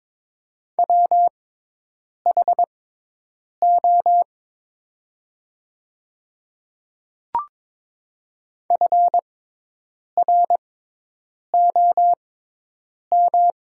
Morse code, heard as a keyed tone, about 22 wpm, spaced as for 8 wpm, 700 Hz. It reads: WHO FROM
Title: 6x Character Spacing - Top 100 Words - 22wpm